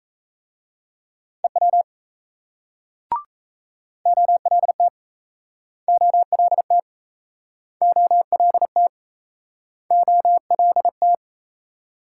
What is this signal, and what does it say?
Morse code, heard as a keyed tone, about 38 wpm, 700 Hz. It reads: EW OLT OLT OLT OLT